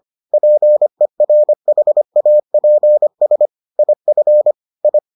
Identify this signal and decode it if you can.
Morse code, heard as a keyed tone, about 25 wpm, 600 Hz. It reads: PERHAPS IF I